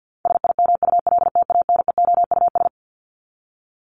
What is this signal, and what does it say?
Morse code, heard as a keyed tone, about 50 wpm, 700 Hz. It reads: SIMULTANEOUS
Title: LY Suffix - Top 100 Words with Base Words 50wpm